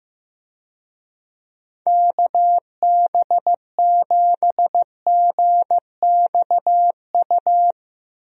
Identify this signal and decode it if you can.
Morse code, heard as a keyed tone, about 15 wpm, 700 Hz. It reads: KB7GXU
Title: USA Call Signs - 6 Characters - Repeated 3x - 15wpm